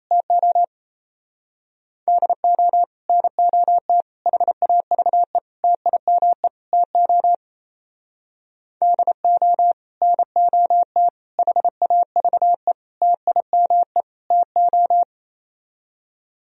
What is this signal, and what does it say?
Morse code, heard as a keyed tone, about 34 wpm, 700 Hz. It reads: TO DO NOT HAVE TIME TO DO NOT HAVE TIME TO
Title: Speed-Racing - Sets of 5 Words 28wpm